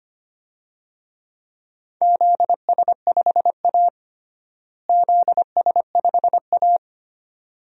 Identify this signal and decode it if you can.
Morse code, heard as a keyed tone, about 25 wpm, 700 Hz. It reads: ZS5A ZS5A
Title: DXCC-100 Call Signs - 4 Characters - Repeated 2x - 25wpm